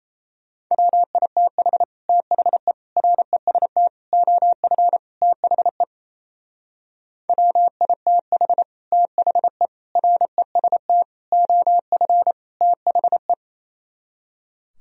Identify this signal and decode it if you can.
Morse code, heard as a keyed tone, about 36 wpm, 700 Hz. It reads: WITH THE REST OF THE WITH THE REST OF THE